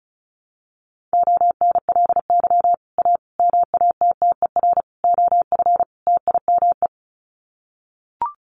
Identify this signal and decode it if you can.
Morse code, heard as a keyed tone, about 34 wpm, 700 Hz. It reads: ONLY A MATTER OF TIME
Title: Sets of 5 Words Encore 35wpm